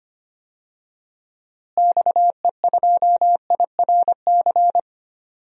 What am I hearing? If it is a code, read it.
Morse code, heard as a keyed tone, about 25 wpm, 700 Hz. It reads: XE2IRC
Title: DXCC-100 Call Signs - 6 Characters - Repeated 3x - 25wpm